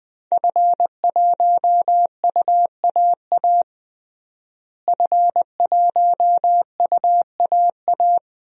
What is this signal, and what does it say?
Morse code, heard as a keyed tone, about 21 wpm, 700 Hz. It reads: F1UAA F1UAA
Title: DXCC-100 Call Signs - 5 Characters - Repeated 3x - 20wpm